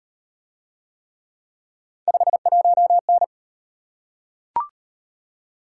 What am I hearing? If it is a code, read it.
Morse code, heard as a keyed tone, about 37 wpm, 700 Hz. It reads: 51N